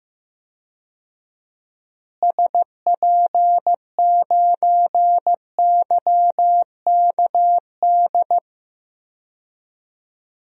Morse code, heard as a keyed tone, about 15 wpm, 700 Hz. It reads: SP9YKD
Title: DXCC-100 Call Signs - 6 Characters - Repeated 2x - 15wpm